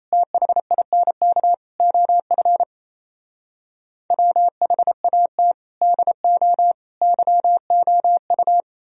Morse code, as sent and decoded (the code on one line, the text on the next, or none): - .... .. -. -.- / --- ..-. / .-- .... .- - / -.. --- / -.-- --- ..-
THINK OF WHAT DO YOU